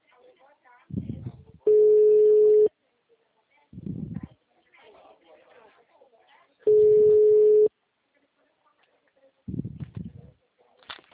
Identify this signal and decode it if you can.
Ringback tone (Europe, 425 Hz), with speech.